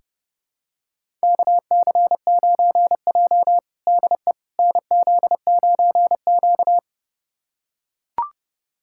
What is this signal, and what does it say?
Morse code, heard as a keyed tone, about 32 wpm, 700 Hz. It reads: KC9J DE NZ9Q
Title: USA Callsign DE Callsign - 4 Characters - 30wpm